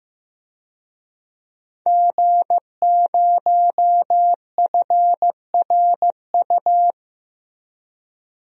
Morse code, heard as a keyed tone, about 15 wpm, 700 Hz. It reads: G0FRU
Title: DXCC-100 Call Signs - 5 Characters - 15wpm